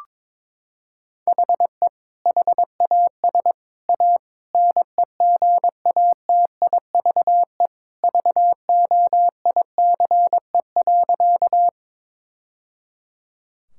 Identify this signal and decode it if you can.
Morse code, heard as a keyed tone, about 22 wpm, 700 Hz. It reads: HE HAS A NEGATIVE VOICE.
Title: Sentences from Top 1000 Words - Review Part 6 - 22wpm